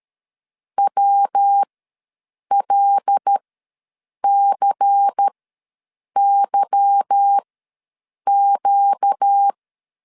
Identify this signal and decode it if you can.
Morse code, heard as a keyed tone, about 13 wpm, spaced as for 7 wpm, 800 Hz. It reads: WLCYQ
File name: LEZIONI / 8 YJ45 / 8 YJ4 ws 12wpm 3249